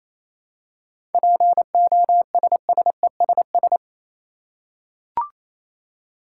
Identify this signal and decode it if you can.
Morse code, heard as a keyed tone, about 28 wpm, 700 Hz. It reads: POSSESS
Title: Seven Letter Words - 28wpm